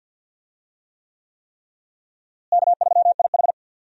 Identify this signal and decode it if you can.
Morse code, heard as a keyed tone, about 47 wpm, 700 Hz. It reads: K3IH
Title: USA Call Signs - 4 Characters - Repeated 3x - 50wpm